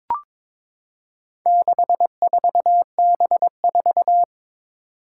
Morse code, heard as a keyed tone, about 22 wpm, 700 Hz. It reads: 64B4
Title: Four Challenging Letter-Number Combinations 22wpm